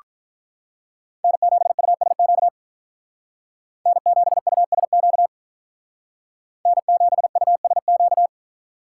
Morse code, heard as a keyed tone, about 45 wpm, 700 Hz. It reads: N7USQ N7USQ N7USQ